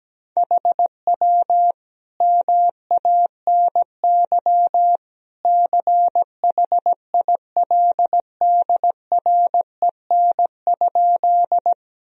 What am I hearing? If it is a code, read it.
Morse code, heard as a keyed tone, about 17 wpm, 700 Hz. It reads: HW MANY CHILDREN?
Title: Sentences from Top 300 Words and Abbreviations 17wpm